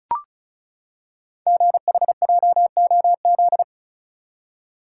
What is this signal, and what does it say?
Morse code, heard as a keyed tone, about 36 wpm, 690 Hz. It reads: GHJOZ
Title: Code Groups - 5 Letters - 35wpm